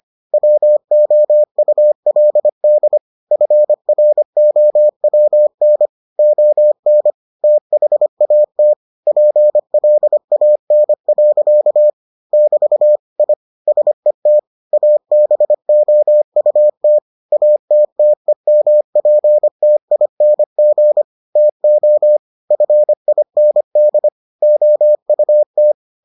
Morse code, heard as a keyed tone, about 25 wpm, 600 Hz. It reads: WOULD FROWN ON THAT PLAN. <BT> I SET ABOUT ATTEMPTING TO FIND OUT